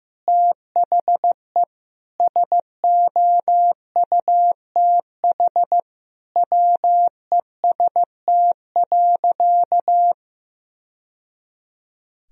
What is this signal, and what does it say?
Morse code, heard as a keyed tone, about 16 wpm, 700 Hz. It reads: THE SOUTH WEST.
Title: Sentences from Top 1000 Words - Review Part 2 - 15wpm